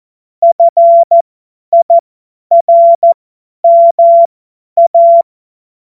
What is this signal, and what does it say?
Morse code, heard as a keyed tone, about 14 wpm, spaced as for 10 wpm, 680 Hz. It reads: FIRMA
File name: Morsekurs / Witze / Witz 20. Dezember 2022 50 F70